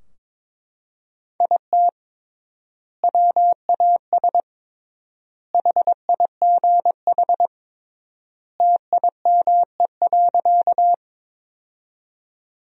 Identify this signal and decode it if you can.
Morse code, heard as a keyed tone, about 22 wpm, 700 Hz. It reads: IT WAS HIGH TIME.
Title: 3x Word-Spacing - Sentences from Top 200 Words - 22wpm